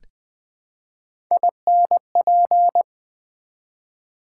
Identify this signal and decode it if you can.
Morse code, heard as a keyed tone, about 20 wpm, 700 Hz. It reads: INP
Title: Build A Word - 1 Letter At A Time - 5-Letter Words - 20wpm